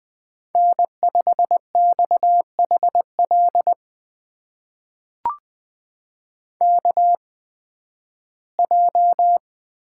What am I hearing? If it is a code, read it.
Morse code, heard as a keyed tone, about 20 wpm, spaced as for 13 wpm, 700 Hz. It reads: N5XHL KJ
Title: USA Call Signs - 8x Character Spacing - 20wpm